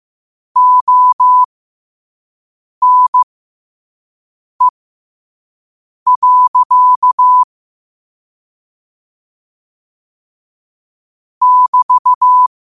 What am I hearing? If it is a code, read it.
Morse code, heard as a keyed tone, about 15 wpm, spaced as for 5 wpm, 1000 Hz. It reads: ONE. <BT>
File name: qotdfiles / 5wpm